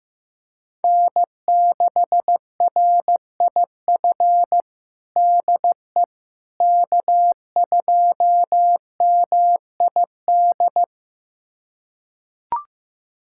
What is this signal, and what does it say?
Morse code, heard as a keyed tone, about 15 wpm, 700 Hz. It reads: N6RIF DE K2MID
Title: USA Callsign DE Callsign - 5 Characters - 15wpm